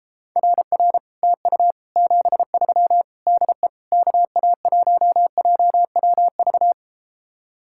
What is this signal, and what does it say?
Morse code, heard as a keyed tone, about 33 wpm, 700 Hz. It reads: RR TU 73 DE KA1JWV